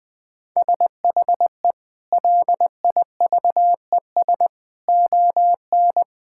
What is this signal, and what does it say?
Morse code, heard as a keyed tone, about 20 wpm, 700 Hz. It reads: SHE LIVES ON
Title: Sentences with 7-Word Phrases - 20wpm